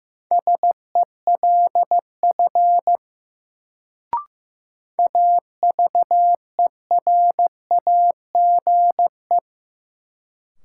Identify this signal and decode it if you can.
Morse code, heard as a keyed tone, about 15 wpm, 700 Hz. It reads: SELF AVERAGE
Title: Sentences from Top 1400 Words - New Word Drill Alt - 15wpm